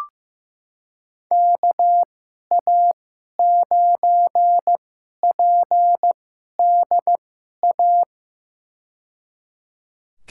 Morse code, heard as a keyed tone, about 15 wpm, spaced as for 11 wpm, 700 Hz. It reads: KA9PDA
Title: USA Call Signs - 2x Character Spacing - 15wpm